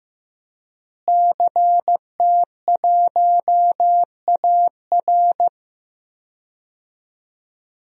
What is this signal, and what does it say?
Morse code, heard as a keyed tone, about 16 wpm, 700 Hz. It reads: CT1AR
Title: DXCC-100 Call Signs - 5 Characters - Repeated 3x - 15wpm